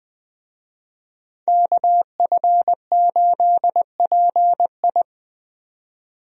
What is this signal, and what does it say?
Morse code, heard as a keyed tone, about 20 wpm, 700 Hz. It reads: KF8PI
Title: USA Call Signs - 5 Characters - 20wpm